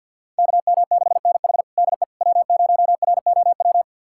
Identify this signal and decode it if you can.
Morse code, heard as a keyed tone, about 51 wpm, 700 Hz. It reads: KM6NH DE W0ROW
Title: USA Callsign DE Callsign - 5 Characters - 50wpm